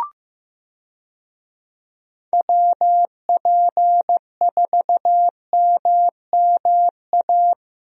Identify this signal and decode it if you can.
Morse code, heard as a keyed tone, about 15 wpm, 700 Hz. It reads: WP4MMA
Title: USA Call Signs - 6 Characters - Repeated 2x - 15wpm